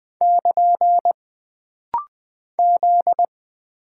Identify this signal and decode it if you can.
Morse code, heard as a keyed tone, about 20 wpm, 700 Hz. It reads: <KN> Z
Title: Single Character (Letter:Number:Punctuation:Prosign) - Rapid-Fire - 20wpm